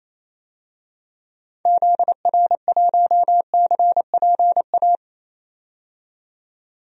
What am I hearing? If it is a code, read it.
Morse code, heard as a keyed tone, about 28 wpm, 700 Hz. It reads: ZR1CPA
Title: DXCC-100 Call Signs - 6 Characters - 28wpm